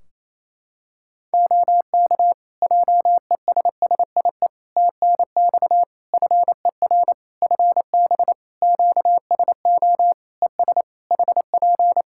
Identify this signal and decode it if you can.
Morse code, heard as a keyed tone, about 27 wpm, 700 Hz. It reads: OK JESSIE TNX FER FB QSO ES HP
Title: Practice QSO - Exchange 4 - Ending - 28wpm